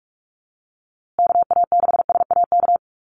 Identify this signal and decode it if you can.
Morse code, heard as a keyed tone, about 44 wpm, 700 Hz. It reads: KA6SAK
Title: USA Call Signs Repeated 2x 45wpm